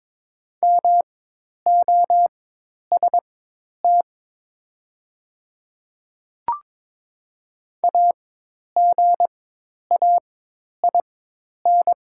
Morse code, heard as a keyed tone, about 22 wpm, spaced as for 10 wpm, 700 Hz. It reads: MOST AGAIN
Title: 4x Character Spacing - Top 300 Words - 22wpm